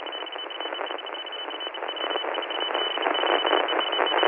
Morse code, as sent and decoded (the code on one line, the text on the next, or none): none